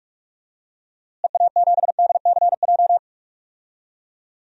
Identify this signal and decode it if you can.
Morse code, heard as a keyed tone, about 44 wpm, 700 Hz. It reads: EA7DCJ
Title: DXCC-100 Call Signs - 6 Characters - Repeated 2x - 45wpm